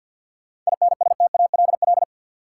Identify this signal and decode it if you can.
Morse code, heard as a keyed tone, about 52 wpm, 700 Hz. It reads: INSTALL
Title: Sentences from Top 1600 Words - New Word Drill Alt - 50wpm